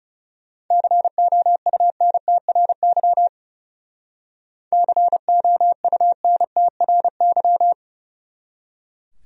Morse code, heard as a keyed tone, about 38 wpm, 700 Hz. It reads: COUNTRY COUNTRY